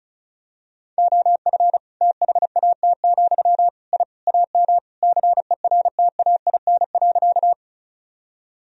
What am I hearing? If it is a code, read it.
Morse code, heard as a keyed tone, about 34 wpm, 700 Hz. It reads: OF THAT, I AM CERTAIN.
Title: Sentences from Top 1000 Words - Review Part 2 - 35wpm